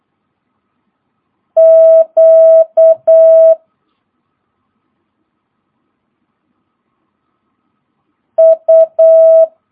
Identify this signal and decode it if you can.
Morse code, heard as a keyed tone, about 8 wpm, 650 Hz. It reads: Q U
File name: jh8cbh / 2022cw